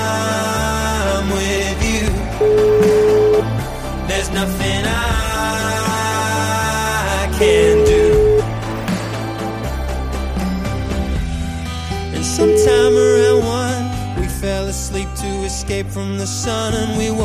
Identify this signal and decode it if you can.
Ringback tone (Europe, 425 Hz), with noise.